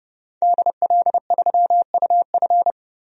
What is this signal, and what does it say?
Morse code, heard as a keyed tone, about 30 wpm, 700 Hz. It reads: DL3UF
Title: DXCC-100 Call Signs - 5 Characters - 30wpm